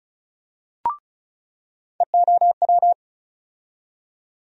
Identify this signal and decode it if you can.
Morse code, heard as a keyed tone, about 34 wpm, 700 Hz. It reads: EOW
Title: USA Call Sign Postfixes - 3 Letters 35wpm